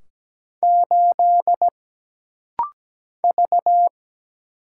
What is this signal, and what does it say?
Morse code, heard as a keyed tone, about 17 wpm, 700 Hz. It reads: 8 V
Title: Single Letter-Number Alt - Rapid-Fire 17wpm